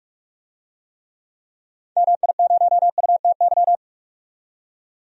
Morse code, heard as a keyed tone, about 46 wpm, 700 Hz. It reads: MI0UTY
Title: DXCC-100 Call Signs - 6 Characters - 45wpm